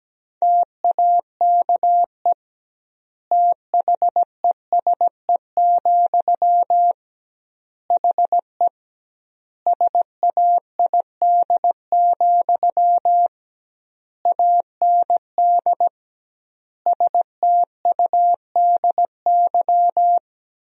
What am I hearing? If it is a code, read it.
Morse code, heard as a keyed tone, about 17 wpm, 700 Hz. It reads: TAKE THESE, HE SAID, AND STUDY
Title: 2x Word-Spacing - Sentences from Top 300 Words - 17wpm